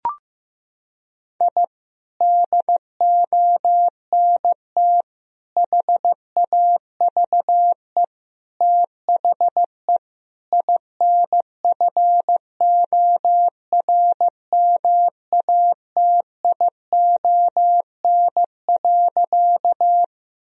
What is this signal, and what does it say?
Morse code, heard as a keyed tone, about 16 wpm, 700 Hz. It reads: I DONT HAVE THE INFORMATION.
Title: Sentences from Top 1000 Words - Review Part 3 - 15wpm